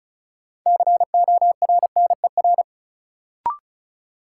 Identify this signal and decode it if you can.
Morse code, heard as a keyed tone, about 36 wpm, 700 Hz. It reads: CORNER
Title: Sentences from Top 1600 Words - New Word Drill Alt - 35wpm